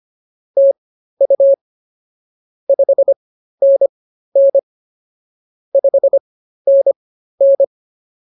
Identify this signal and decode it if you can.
Morse code, heard as a keyed tone, about 24 wpm, spaced as for 13 wpm, 550 Hz. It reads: TU 5NN 5NN